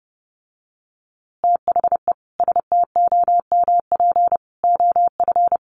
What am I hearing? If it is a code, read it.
Morse code, heard as a keyed tone, about 30 wpm, 700 Hz. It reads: THE STOMP OF
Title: A Sound Of Thunder 30wpm